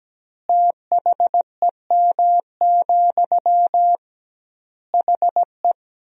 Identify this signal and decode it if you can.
Morse code, heard as a keyed tone, about 17 wpm, 700 Hz. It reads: THEM, HE
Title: 2x Word-Spacing - Sentences from Top 400 Words - 17wpm